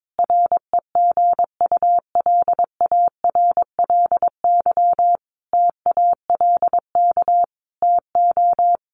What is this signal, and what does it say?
Morse code, heard as a keyed tone, about 22 wpm, 700 Hz. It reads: REGULARLY TALK TO
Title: Sentences from Top 1400 Words 22wpm